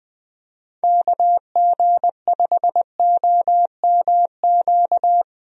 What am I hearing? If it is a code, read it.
Morse code, heard as a keyed tone, about 20 wpm, 700 Hz. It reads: KG5OMQ
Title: USA Call Signs - 6 Characters - Repeated 2x - 20wpm